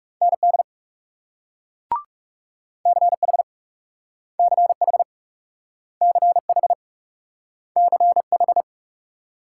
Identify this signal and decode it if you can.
Morse code, heard as a keyed tone, about 42 wpm, 700 Hz. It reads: ND CH CH CH CH